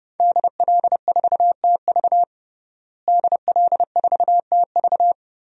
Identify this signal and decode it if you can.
Morse code, heard as a keyed tone, about 28 wpm, 700 Hz. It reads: DL4TV DL4TV
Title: DXCC-100 Call Signs - 5 Characters - Repeated 3x - 30wpm